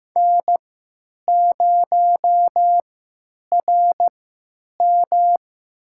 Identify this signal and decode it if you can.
Morse code, heard as a keyed tone, about 15 wpm, spaced as for 9 wpm, 700 Hz. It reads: N0RM
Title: USA Call Signs - 4 Characters - 3x Character Spacing - 15wpm